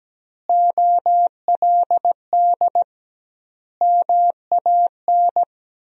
Morse code, heard as a keyed tone, about 17 wpm, 700 Hz. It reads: OLD MAN
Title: 2x Word-Spacing - Sentences from Top 200 Words - 17wpm